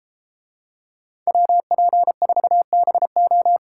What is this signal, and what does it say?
Morse code, heard as a keyed tone, about 32 wpm, 700 Hz. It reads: WP4BO